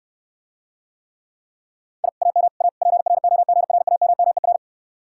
Morse code, heard as a keyed tone, about 69 wpm, 700 Hz. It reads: I AM A PROGRAMMER